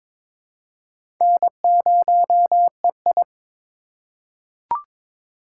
Morse code, heard as a keyed tone, about 22 wpm, 700 Hz. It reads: N0EI
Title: USA Call Signs - 4 Characters - 3x Character Spacing - 22wpm